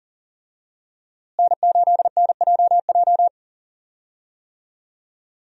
Morse code, heard as a keyed tone, about 40 wpm, 700 Hz. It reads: N8NJJ
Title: USA Call Signs - 5 Characters - Repeated 2x - 40wpm